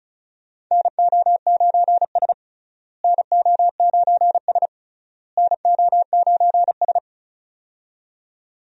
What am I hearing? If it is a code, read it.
Morse code, heard as a keyed tone, about 36 wpm, 700 Hz. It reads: NO9S NO9S NO9S
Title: USA Call Signs - 4 Characters - Repeated 3x - 35wpm